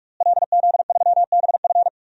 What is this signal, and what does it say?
Morse code, heard as a keyed tone, about 45 wpm, 700 Hz. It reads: LZ3BF
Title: DXCC-100 Call Signs - 5 Characters - Repeated 3x - 45wpm